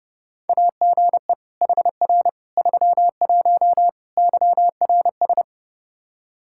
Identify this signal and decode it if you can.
Morse code, heard as a keyed tone, about 30 wpm, 700 Hz. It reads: AGE HR 31 YRS
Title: Practice QSO - Exchange 3 - More Info - 30wpm